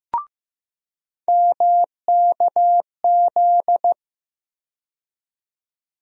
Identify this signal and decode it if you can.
Morse code, heard as a keyed tone, about 15 wpm, 700 Hz. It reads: MKZ
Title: USA Call Sign Postfixes - 3 Letters 15wpm